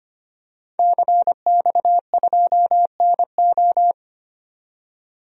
Morse code, heard as a keyed tone, about 25 wpm, 700 Hz. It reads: CX2NO
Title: DXCC-100 Call Signs - 5 Characters - 25wpm